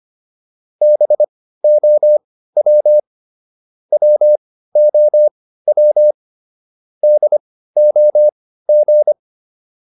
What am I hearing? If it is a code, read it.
Morse code, heard as a keyed tone, about 25 wpm, spaced as for 15 wpm, 600 Hz. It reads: BOW WOW DOG